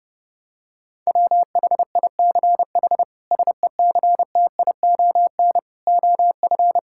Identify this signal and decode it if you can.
Morse code, heard as a keyed tone, about 30 wpm, 700 Hz. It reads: WHICH SECTION OF
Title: Sentences from Top 1000 Words - Review Part 5 - 30wpm